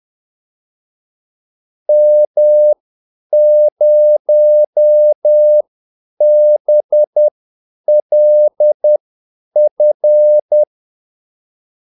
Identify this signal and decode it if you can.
Morse code, heard as a keyed tone, about 10 wpm, spaced as for 8 wpm, 600 Hz. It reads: M0BLF
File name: Callsigns-10-8wpm